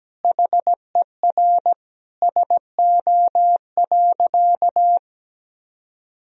Morse code, heard as a keyed tone, about 17 wpm, 700 Hz. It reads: HER SO.
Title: Sentences from Top 1000 Words - Review Part 2 - 17wpm